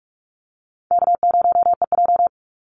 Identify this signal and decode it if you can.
Morse code, heard as a keyed tone, about 44 wpm, 700 Hz. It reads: K0EJ
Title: Easy Call Signs Repeated 2x - 45wpm